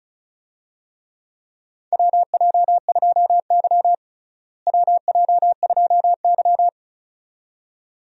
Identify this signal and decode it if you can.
Morse code, heard as a keyed tone, about 36 wpm, 700 Hz. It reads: WJ2Y WJ2Y
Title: USA Call Signs - 4 Characters - Repeated 2x - 35wpm